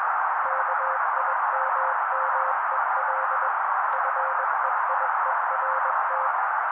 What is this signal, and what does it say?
Morse code, heard as a keyed tone, about 21 wpm, 530 Hz. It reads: KIMMEL FEIERT
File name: daily / 20wpm noise